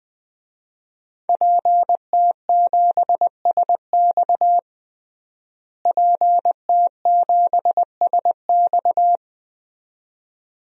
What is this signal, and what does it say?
Morse code, heard as a keyed tone, about 20 wpm, 700 Hz. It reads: PT7SX PT7SX
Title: DXCC-100 Call Signs - 5 Characters - Repeated 2x - 20wpm